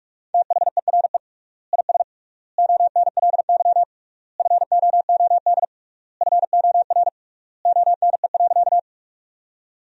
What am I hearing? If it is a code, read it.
Morse code, heard as a keyed tone, about 46 wpm, 700 Hz. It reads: THERE IS ONLY FOOD FOR ONE.
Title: 3x Word-Spacing - Sentences from Top 300 Words - 45wpm